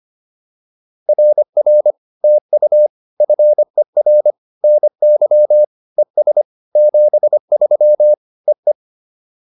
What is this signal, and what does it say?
Morse code, heard as a keyed tone, about 25 wpm, 590 Hz. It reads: RR TU FER NY ES 73 EE